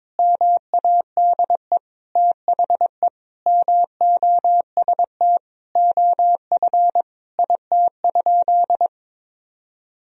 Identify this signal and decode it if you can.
Morse code, heard as a keyed tone, about 22 wpm, 700 Hz. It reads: MADE THE MOST OF IT?
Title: Sentences from Top 1000 Words - Review Part 1 - 22wpm